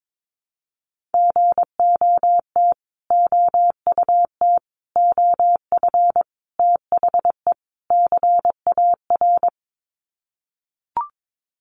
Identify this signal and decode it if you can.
Morse code, heard as a keyed tone, about 22 wpm, 700 Hz. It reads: GOT OUT OF THE CAR
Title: Sets of 5 Words Encore 22wpm